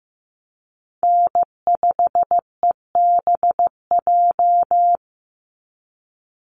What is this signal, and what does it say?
Morse code, heard as a keyed tone, about 15 wpm, 700 Hz. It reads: N5EBJ
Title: USA Call Signs 15wpm